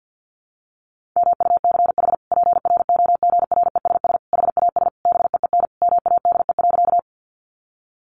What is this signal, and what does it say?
Morse code, heard as a keyed tone, about 50 wpm, 700 Hz. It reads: MUCH PROGRESS HAS BEEN MADE.
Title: Sentences from Top 1400 Words 50wpm